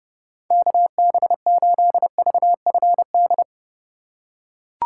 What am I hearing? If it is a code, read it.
Morse code, heard as a keyed tone, about 29 wpm, 700 Hz. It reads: KB8VFD